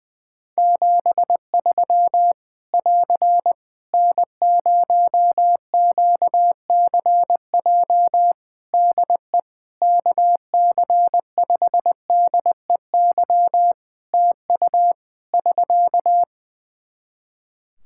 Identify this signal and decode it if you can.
Morse code, heard as a keyed tone, about 20 wpm, 700 Hz. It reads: 73 <AR> N0QCJ DE KC5DEY TU <SK>
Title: Practice QSO - Exchange 4 - Ending - 20wpm